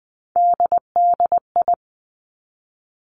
Morse code, heard as a keyed tone, about 20 wpm, 700 Hz. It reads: DDI
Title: Three Challenging Letter-Number Combinations 20wpm